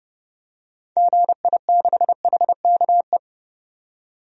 Morse code, heard as a keyed tone, about 29 wpm, 700 Hz. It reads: GI6HKE
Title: DXCC-100 Call Signs - 6 Characters - 30wpm